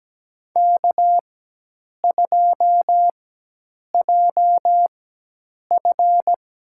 Morse code, heard as a keyed tone, about 17 wpm, spaced as for 8 wpm, 700 Hz. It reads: K2JF
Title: USA Call Signs - 4x Character Spacing - 17wpm